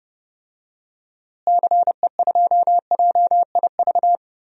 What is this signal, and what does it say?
Morse code, heard as a keyed tone, about 31 wpm, 700 Hz. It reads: CE2JIV
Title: DXCC-100 Call Signs - 6 Characters - Repeated 3x - 30wpm